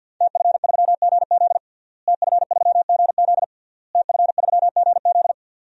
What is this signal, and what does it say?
Morse code, heard as a keyed tone, about 49 wpm, 700 Hz. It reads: TF3GZ TF3GZ TF3GZ
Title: DXCC-100 Call Signs - 5 Characters - Repeated 3x - 50wpm